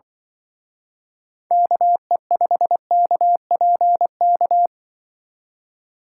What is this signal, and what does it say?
Morse code, heard as a keyed tone, about 24 wpm, 700 Hz. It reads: KE5KPK
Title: Speed Racing - USA Call Signs 20wpm